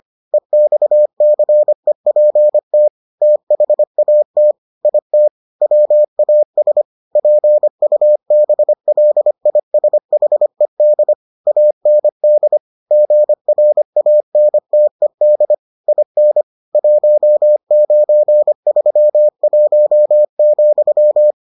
Morse code, heard as a keyed tone, about 25 wpm, 600 Hz. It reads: EXCEPT THAT IT WAS PUBLISHED AND GRANTED IN 1931,